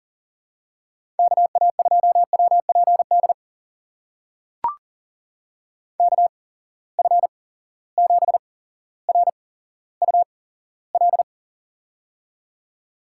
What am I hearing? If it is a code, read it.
Morse code, heard as a keyed tone, about 43 wpm, spaced as for 17 wpm, 700 Hz. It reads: KA2WPD KF7RUL
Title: USA Call Signs - 6 Characters - 8x Character Spacing - 40wpm